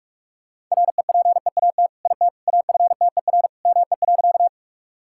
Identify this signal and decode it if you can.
Morse code, heard as a keyed tone, about 41 wpm, 700 Hz. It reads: REPEAT IT AFTER ME.
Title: Sentences from Top 1600 Words 45wpm